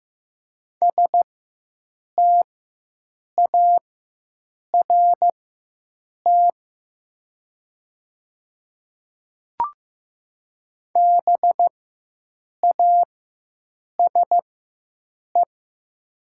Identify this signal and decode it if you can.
Morse code, heard as a keyed tone, about 15 wpm, spaced as for 7 wpm, 700 Hz. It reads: START BASE
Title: 4x Character Spacing - Top 300 Words - 15wpm